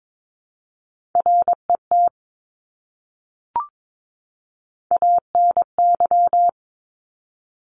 Morse code, heard as a keyed tone, about 22 wpm, 700 Hz. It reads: RET ANY
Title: Top Three Letter Combinations in Words 22wpm